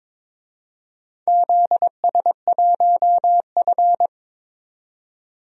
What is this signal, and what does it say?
Morse code, heard as a keyed tone, about 22 wpm, 700 Hz. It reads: ZS1F
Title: DXCC-100 Call Signs - 4 Characters - 22wpm